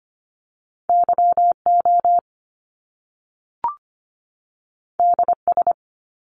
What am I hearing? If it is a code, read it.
Morse code, heard as a keyed tone, about 25 wpm, 700 Hz. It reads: YO DS
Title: Top Two Letter Combinations in Words 25wpm